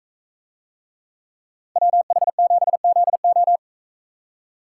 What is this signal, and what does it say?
Morse code, heard as a keyed tone, about 42 wpm, 700 Hz. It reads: WH7ZO